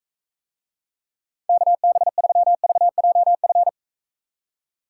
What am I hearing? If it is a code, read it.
Morse code, heard as a keyed tone, about 43 wpm, 700 Hz. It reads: KB3VJF